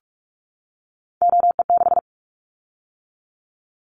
Morse code, heard as a keyed tone, about 46 wpm, 700 Hz. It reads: OE6
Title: Callsign Prefixes for Top 50 DXCC - 45wpm